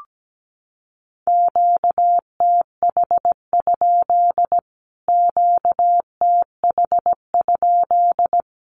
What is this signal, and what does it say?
Morse code, heard as a keyed tone, about 17 wpm, 700 Hz. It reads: QTH? QTH?
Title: Q-Codes Repeated 3x 17wpm